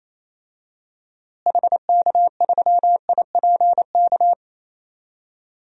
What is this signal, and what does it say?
Morse code, heard as a keyed tone, about 28 wpm, 700 Hz. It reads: HK3IPK
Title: DXCC-100 Call Signs - 6 Characters - Repeated 3x - 28wpm